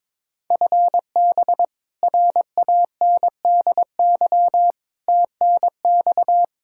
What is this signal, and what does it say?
Morse code, heard as a keyed tone, about 22 wpm, 700 Hz. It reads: FB RANDY TNX
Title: Practice QSO - Exchange 3 - More Info - 22wpm